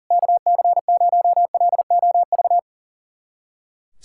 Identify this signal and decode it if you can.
Morse code, heard as a keyed tone, about 40 wpm, 700 Hz. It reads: KC0LOV
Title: SOTA QSO Practice 40wpm